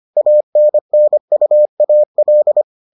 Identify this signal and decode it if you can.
Morse code, heard as a keyed tone, about 25 wpm, 600 Hz. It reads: ANNUAL